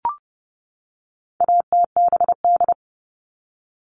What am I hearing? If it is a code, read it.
Morse code, heard as a keyed tone, about 30 wpm, 700 Hz. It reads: ATBD
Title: All Cut Numbers - 4-Digit 30wpm